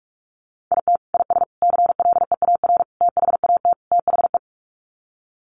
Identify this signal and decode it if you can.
Morse code, heard as a keyed tone, about 44 wpm, 700 Hz. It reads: IT IS CLEAR THAT THE
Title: Sets of 5 Words Encore 45wpm